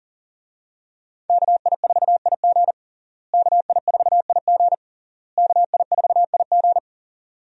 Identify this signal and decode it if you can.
Morse code, heard as a keyed tone, about 37 wpm, 700 Hz. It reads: KI4IG KI4IG KI4IG
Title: USA Call Signs - 5 Characters - Repeated 3x - 40wpm